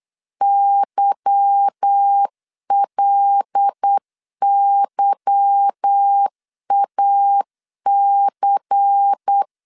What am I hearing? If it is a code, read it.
Morse code, heard as a keyed tone, about 8 wpm, 800 Hz. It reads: YLYAC
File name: LEZIONI / 8 YJ45 / 8 Y 8wpm 3245